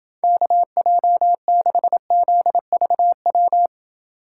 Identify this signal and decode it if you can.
Morse code, heard as a keyed tone, about 29 wpm, 700 Hz. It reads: KJ6ZVW